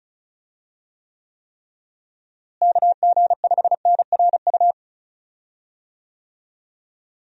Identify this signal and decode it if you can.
Morse code, heard as a keyed tone, about 36 wpm, 700 Hz. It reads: KG5NRU
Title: USA Call Signs - 6 Characters - Repeated 3x - 35wpm